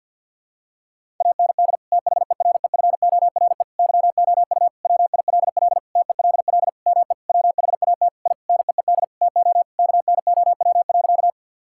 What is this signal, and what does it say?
Morse code, heard as a keyed tone, about 52 wpm, 700 Hz. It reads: AND THEREFORE YOU WILL TELL ME WHAT I NEED TO KNOW.
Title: Sentences from Top 1000 Words - Review Part 5 - 50wpm